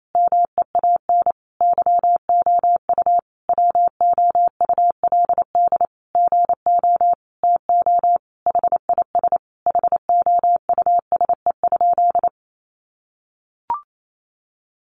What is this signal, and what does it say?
Morse code, heard as a keyed tone, about 28 wpm, 700 Hz. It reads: MEAN YOU WOULD GO TO HIS HOUSE?
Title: Sentences from Top 600 Words 28wpm